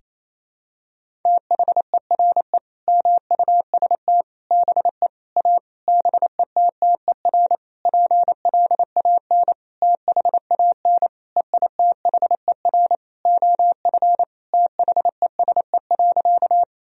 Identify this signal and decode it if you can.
Morse code, heard as a keyed tone, about 28 wpm, 700 Hz. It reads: THERE MUST BE A BETTER PLAN THAN EITHER OF THESE.
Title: Sentences from Top 1000 Words - Review Part 4 - 28wpm